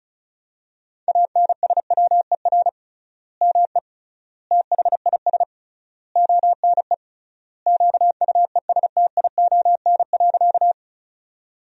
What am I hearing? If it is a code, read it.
Morse code, heard as a keyed tone, about 34 wpm, 700 Hz. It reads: ANSWER ME THIS ONE QUESTION.
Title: 3x Word-Spacing - Sentences from Top 400 Words - 35wpm